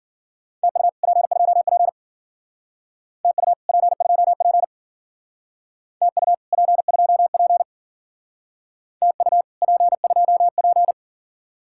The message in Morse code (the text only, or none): TU P2P TU P2P TU P2P TU P2P